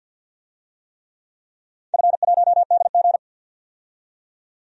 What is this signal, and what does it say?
Morse code, heard as a keyed tone, about 48 wpm, 700 Hz. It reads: F1DG
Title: DXCC-100 Call Signs - 4 Characters - 50wpm